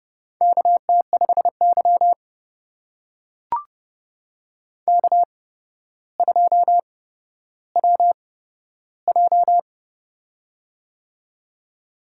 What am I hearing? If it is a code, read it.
Morse code, heard as a keyed tone, about 30 wpm, spaced as for 13 wpm, 700 Hz. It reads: KT5Y K2WJ
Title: USA Call Signs - 4 Characters - 8x Character Spacing - 30wpm